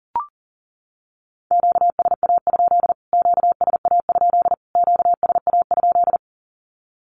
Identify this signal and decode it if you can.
Morse code, heard as a keyed tone, about 37 wpm, 700 Hz. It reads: QSA? QSA? QSA?
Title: Q-Codes Repeated 3x 40wpm